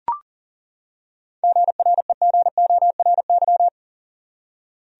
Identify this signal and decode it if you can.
Morse code, heard as a keyed tone, about 40 wpm, 700 Hz. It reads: GREGORY
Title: USA Common Names - 40wpm